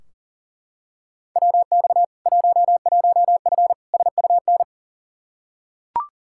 Morse code, heard as a keyed tone, about 38 wpm, 700 Hz. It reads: WX 11F SUN
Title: Practice QSO - Exchange 2 - Info - 40wpm